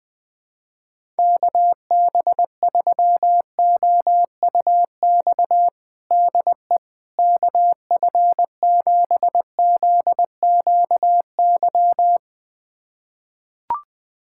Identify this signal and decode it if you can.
Morse code, heard as a keyed tone, about 20 wpm, 700 Hz. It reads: KB3OUX DE KF7ZQY
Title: USA Callsign DE Callsign - 6 Characters - 20wpm